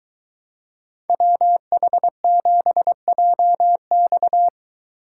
Morse code, heard as a keyed tone, about 23 wpm, 700 Hz. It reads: WH7JX